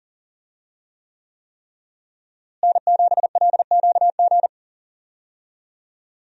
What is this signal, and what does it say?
Morse code, heard as a keyed tone, about 40 wpm, 700 Hz. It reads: N7LQG
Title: USA Call Signs - 5 Characters - Repeated 3x - 40wpm